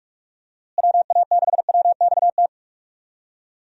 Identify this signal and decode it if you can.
Morse code, heard as a keyed tone, about 44 wpm, 700 Hz. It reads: WA6WXT
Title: USA Call Signs - 6 Characters - Repeated 2x - 45wpm